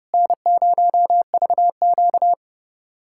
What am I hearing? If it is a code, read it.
Morse code, heard as a keyed tone, about 32 wpm, 700 Hz. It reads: N0VQ
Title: USA Callsign DE Callsign - 4 Characters - 30wpm